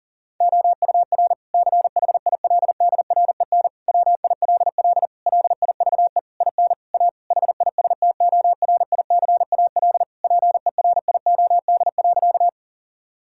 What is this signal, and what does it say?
Morse code, heard as a keyed tone, about 40 wpm, 700 Hz. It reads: OUR CHILDREN WILL LIVE IN A HISTORICAL PERIOD.
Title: Sentences from Top 1000 Words - Review Part 5 - 40wpm